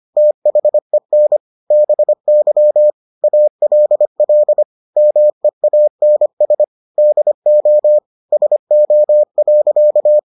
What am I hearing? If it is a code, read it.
Morse code, heard as a keyed tone, about 25 wpm, 600 Hz. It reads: THEN BY ALL MEANS DO SO.